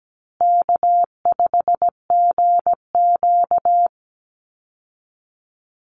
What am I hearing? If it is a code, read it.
Morse code, heard as a keyed tone, about 17 wpm, 700 Hz. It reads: K5GQ
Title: Easy Call Signs Repeated 3x - 17wpm